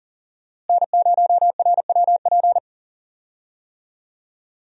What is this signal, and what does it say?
Morse code, heard as a keyed tone, about 40 wpm, 700 Hz. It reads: N0RWP
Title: USA Call Signs - 5 Characters - Repeated 2x - 40wpm